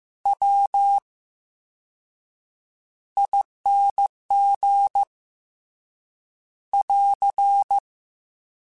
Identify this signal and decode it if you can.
Morse code, heard as a keyed tone, about 15 wpm, spaced as for 7 wpm, 800 Hz. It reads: WING <AR>
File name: Special INGs / Special INGs-mp3s / Slow ing @13